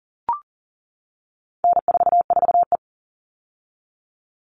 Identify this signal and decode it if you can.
Morse code, heard as a keyed tone, about 40 wpm, 700 Hz. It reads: N44E
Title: All Cut Numbers - 4-Digit 40wpm